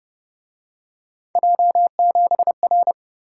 Morse code, heard as a keyed tone, about 29 wpm, 700 Hz. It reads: J7R
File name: Speed-Racing - Three Letter-Number Combinations - 25wpm